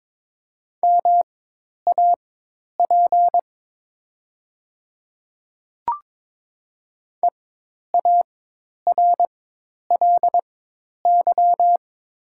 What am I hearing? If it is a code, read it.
Morse code, heard as a keyed tone, about 22 wpm, spaced as for 10 wpm, 700 Hz. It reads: MAP EARLY
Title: 4x Character Spacing - Top 500 Words - 22wpm